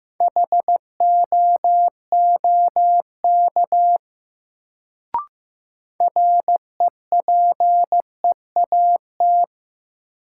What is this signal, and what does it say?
Morse code, heard as a keyed tone, about 15 wpm, 700 Hz. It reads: HOOK REPEAT
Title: Sentences from Top 1600 Words - New Word Drill Alt - 15wpm